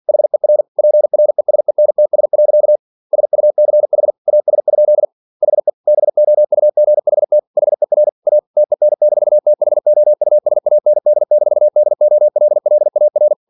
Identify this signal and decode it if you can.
Morse code, heard as a keyed tone, about 48 wpm, 600 Hz. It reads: HER PRESENTS. SUCH AS? HE BOUGHT HER A TEN-THOUSAND-DOLLAR
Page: Never 47wpm